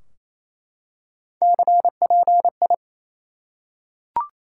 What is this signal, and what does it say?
Morse code, heard as a keyed tone, about 28 wpm, 700 Hz. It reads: CPI
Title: Top 90 QSO Elements 28wpm